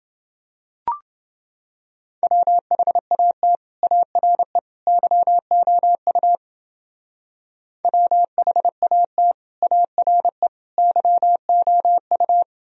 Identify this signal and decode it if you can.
Morse code, heard as a keyed tone, about 28 wpm, 700 Hz. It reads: WHAT ARE YOU WHAT ARE YOU